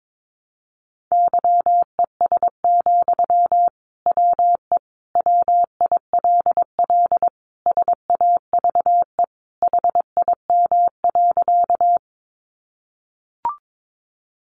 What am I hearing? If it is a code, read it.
Morse code, heard as a keyed tone, about 22 wpm, 700 Hz. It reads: YES, WE WILL SAVE HIM.
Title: Sentences from Top 1100 Words 22wpm